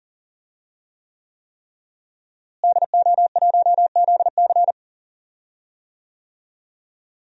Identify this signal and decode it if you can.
Morse code, heard as a keyed tone, about 37 wpm, 700 Hz. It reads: DO1ZC
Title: DXCC-100 Call Signs - 5 Characters - Repeated 3x - 40wpm